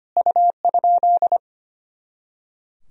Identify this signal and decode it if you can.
Morse code, heard as a keyed tone, about 25 wpm, 700 Hz. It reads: U?
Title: Sentences from Top 100 Words and Abbreviations 25wpm